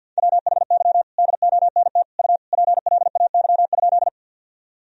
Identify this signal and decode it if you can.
Morse code, heard as a keyed tone, about 47 wpm, 700 Hz. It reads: WHY DONT U PLAY?
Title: Sentences from Top 300 Words and Abbreviations 50wpm